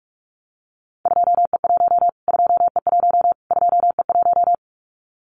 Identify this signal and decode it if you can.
Morse code, heard as a keyed tone, about 44 wpm, 700 Hz. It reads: 2E1 2E1 2E1
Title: Callsign Prefixes for Top 25 DXCC - Repeated 3x - 45wpm